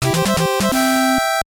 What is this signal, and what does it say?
There is no code.